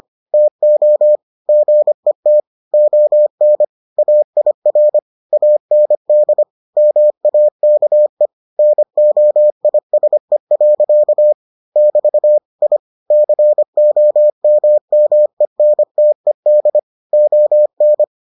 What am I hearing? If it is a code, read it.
Morse code, heard as a keyed tone, about 25 wpm, 600 Hz. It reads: TO GET ON AIR AND MAKE NOISE. <BT> I COMMENTED ON